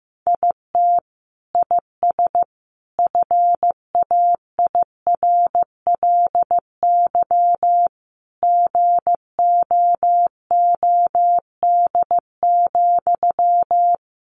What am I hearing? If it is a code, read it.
Morse code, heard as a keyed tone, about 15 wpm, 700 Hz. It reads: IT IS FAIRLY GOOD,
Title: Sentences from Top 1100 Words 15wpm